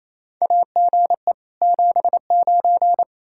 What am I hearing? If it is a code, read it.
Morse code, heard as a keyed tone, about 28 wpm, 700 Hz. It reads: AGE 79
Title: Practice QSO - Exchange 3 - More Info - 28wpm